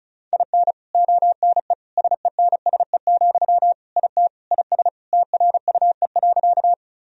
Morse code, heard as a keyed tone, about 34 wpm, 700 Hz. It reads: IN ONE SENSE, IT IS TRUE.
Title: Sentences from Top 1000 Words - Review Part 4 - 35wpm